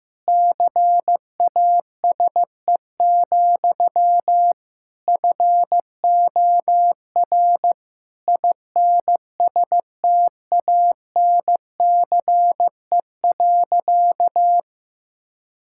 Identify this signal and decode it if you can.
Morse code, heard as a keyed tone, about 15 wpm, 700 Hz. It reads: CASE, FOR INSTANCE.
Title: Sentences from Top 1000 Words - Review Part 5 - 15wpm